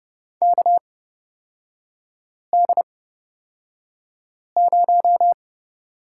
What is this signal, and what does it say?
Morse code, heard as a keyed tone, about 30 wpm, spaced as for 5 wpm, 700 Hz. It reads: KD0
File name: Lesson 36e - BT - Three Characters 30wpm